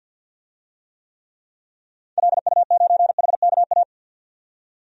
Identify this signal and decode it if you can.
Morse code, heard as a keyed tone, about 53 wpm, 700 Hz. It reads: LU9HKA
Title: DXCC-100 Call Signs - 6 Characters - 50wpm